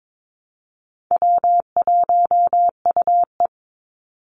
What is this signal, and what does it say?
Morse code, heard as a keyed tone, about 22 wpm, 700 Hz. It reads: W1UE
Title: Easy Call Signs Repeated 3x - 22wpm